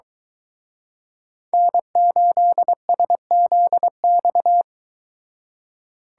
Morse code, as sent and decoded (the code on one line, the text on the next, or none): -. ---.. ... --.. -..-
N8SZX